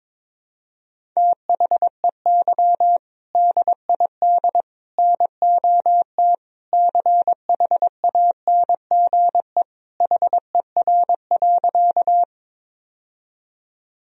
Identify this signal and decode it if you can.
Morse code, heard as a keyed tone, about 22 wpm, 700 Hz. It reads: THEY DID NOT CHANGE HER.
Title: Sentences from Top 1000 Words - Review Part 1 - 22wpm